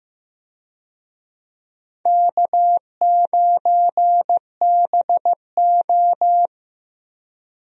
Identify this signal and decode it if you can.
Morse code, heard as a keyed tone, about 15 wpm, 700 Hz. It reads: K9BO
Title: USA Call Signs - 4 Characters - Repeated 2x - 15wpm